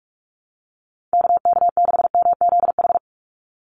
Morse code, heard as a keyed tone, about 44 wpm, 700 Hz. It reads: KK6MZH
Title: USA Call Signs 45wpm